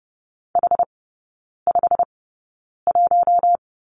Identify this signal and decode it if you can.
Morse code, heard as a keyed tone, about 30 wpm, spaced as for 9 wpm, 700 Hz. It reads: H51
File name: course / 30:10wpm / 087 Lesson 15e - Number 5 - Three Characters 30wpm